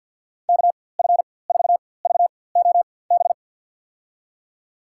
Morse code, heard as a keyed tone, about 49 wpm, spaced as for 23 wpm, 700 Hz. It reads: KF4VOB
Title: USA Call Signs - 6 Characters - 4x Character Spacing - 50wpm